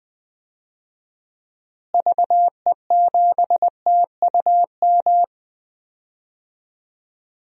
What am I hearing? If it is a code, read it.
Morse code, heard as a keyed tone, about 20 wpm, 700 Hz. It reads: VE7TUM
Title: DXCC-100 Call Signs - 6 Characters - Repeated 2x - 20wpm